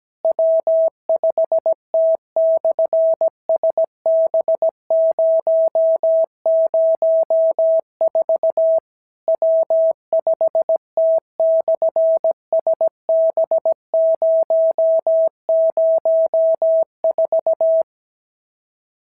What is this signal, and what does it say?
Morse code, heard as a keyed tone, about 17 wpm, 650 Hz. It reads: W5T/SB004 W5T/SB004
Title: SOTA QSO Practice 17wpm